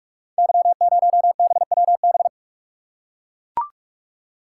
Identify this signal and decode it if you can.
Morse code, heard as a keyed tone, about 44 wpm, 690 Hz. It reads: Y0BWB
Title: Code Groups - 5 Letter:Numbers - 45wpm